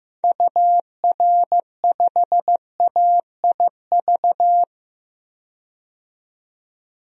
Morse code, heard as a keyed tone, about 15 wpm, 700 Hz. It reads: UR5AIV
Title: DXCC-100 Call Signs - 6 Characters - 15wpm